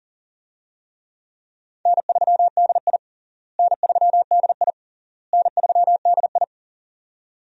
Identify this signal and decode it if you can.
Morse code, heard as a keyed tone, about 40 wpm, 700 Hz. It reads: N3DI N3DI N3DI
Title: USA Call Signs - 4 Characters - Repeated 3x - 40wpm